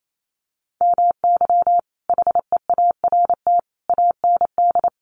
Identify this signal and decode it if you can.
Morse code, heard as a keyed tone, about 28 wpm, 700 Hz. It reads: MY HEART AND
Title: Sentences from Top 1200 Words 28wpm